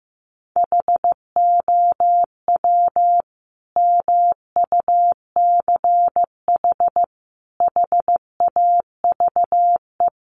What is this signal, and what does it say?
Morse code, heard as a keyed tone, about 15 wpm, 700 Hz. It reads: HOW MUCH HAVE
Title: Sentences from Top 300 Words 15wpm